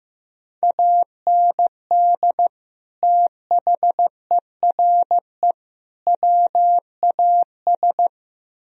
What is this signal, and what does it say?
Morse code, heard as a keyed tone, about 17 wpm, 700 Hz. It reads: AND THERE WAS
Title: Speed-Racing - Sets of 3 Words 15wpm